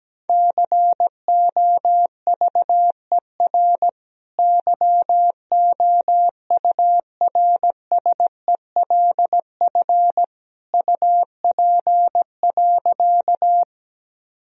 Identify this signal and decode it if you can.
Morse code, heard as a keyed tone, about 17 wpm, 700 Hz. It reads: COVER YOURSELF UP.
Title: Sentences from Top 1000 Words - Review Part 1 - 17wpm